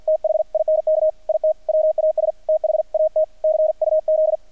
Morse code, heard as a keyed tone, about 48 wpm, 610 Hz, with radio noise. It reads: THINK IT WAS THAT CUZ
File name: archived / CRSnet